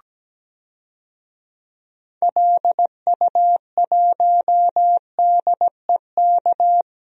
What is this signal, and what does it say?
Morse code, heard as a keyed tone, about 17 wpm, 700 Hz. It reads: LU1DEK
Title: DXCC-100 Call Signs - 6 Characters - Repeated 3x - 17wpm